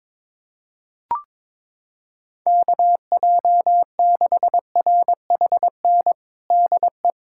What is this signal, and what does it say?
Morse code, heard as a keyed tone, about 22 wpm, 700 Hz. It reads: KJ6RHN DE
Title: USA Callsign DE Callsign 22wpm